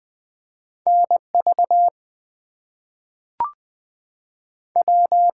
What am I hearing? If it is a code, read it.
Morse code, heard as a keyed tone, about 20 wpm, 700 Hz. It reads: NV W
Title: US States and Canadian Provinces 20wpm